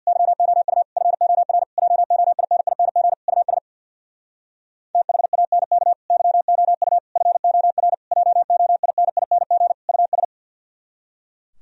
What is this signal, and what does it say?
Morse code, heard as a keyed tone, about 64 wpm, 700 Hz. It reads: YOU FOR JOINING US THANK YOU FOR JOINING US